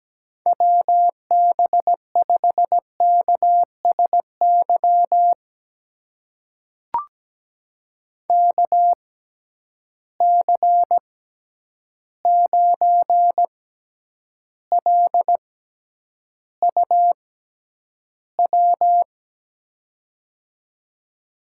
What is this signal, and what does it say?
Morse code, heard as a keyed tone, about 17 wpm, spaced as for 9 wpm, 700 Hz. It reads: WB5KSY KC9LUW
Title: USA Call Signs - 6 Characters - 6x Character Spacing - 17wpm